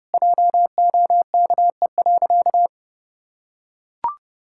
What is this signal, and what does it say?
Morse code, heard as a keyed tone, about 32 wpm, 700 Hz. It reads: JOKE.
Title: Pangram Challenge v2 - 30wpm